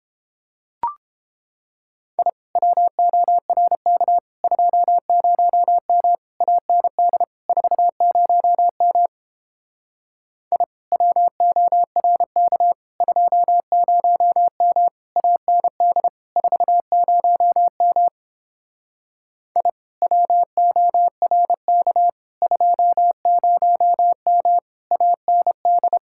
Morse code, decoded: I WORK 20M AND 40M I WORK 20M AND 40M I WORK 20M AND

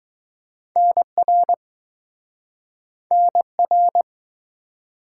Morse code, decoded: NR NR

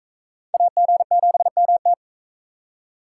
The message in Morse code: AG7MT

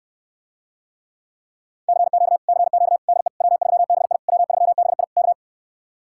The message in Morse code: CQ CQ DE W3BI W3BI K

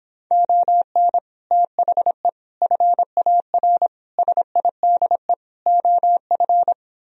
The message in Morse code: ON THE FAR SIDE OF

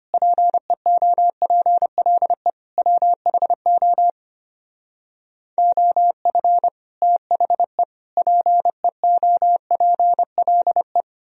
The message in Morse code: PEOPLE WHO OF THE PEOPLE

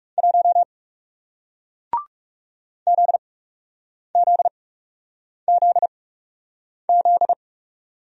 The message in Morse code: 1 ZZZZ

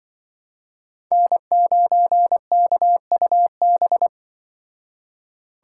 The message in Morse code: N9KUB

